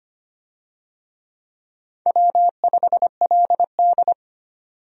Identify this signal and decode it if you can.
Morse code, heard as a keyed tone, about 25 wpm, 700 Hz. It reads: W5LD
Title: USA Call Signs - 4 Characters - 25wpm